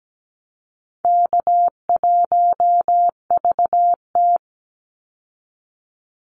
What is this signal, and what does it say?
Morse code, heard as a keyed tone, about 17 wpm, 700 Hz. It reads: K1VT
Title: Easy Call Signs Repeated 2x - 17wpm